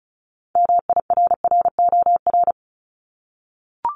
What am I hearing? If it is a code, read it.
Morse code, heard as a keyed tone, about 36 wpm, 700 Hz. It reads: MIRROR
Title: Multi-letter Phonemes 35wpm